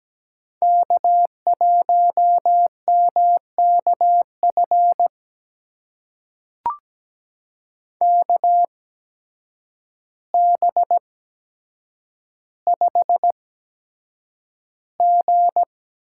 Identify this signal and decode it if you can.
Morse code, heard as a keyed tone, about 17 wpm, spaced as for 8 wpm, 700 Hz. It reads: K1MKF KB5G
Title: USA Call Signs - 5 Characters - 8x Character Spacing - 17wpm